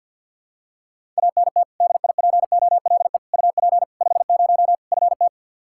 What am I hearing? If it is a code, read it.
Morse code, heard as a keyed tone, about 51 wpm, 700 Hz. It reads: ANT DIPOLE UP 50 FT
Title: Practice QSO - Exchange 2 - Info - 50wpm